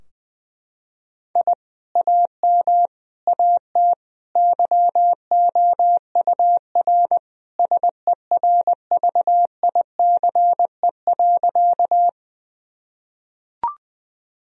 Morse code, heard as a keyed tone, about 20 wpm, 700 Hz. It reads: I AM AT YOUR SERVICE.
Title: Sentences from Top 1000 Words - Review Part 4 - 20wpm